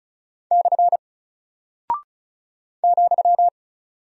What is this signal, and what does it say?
Morse code, heard as a keyed tone, about 36 wpm, 700 Hz. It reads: / ,
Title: Single Character (Letter:Number:Punctuation:Prosign) Alt 35wpm